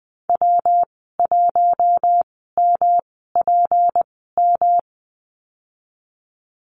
Morse code, heard as a keyed tone, about 20 wpm, spaced as for 15 wpm, 700 Hz. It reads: W1MPM
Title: 231 Lesson 37g - Letter x - Call Signs 20wpm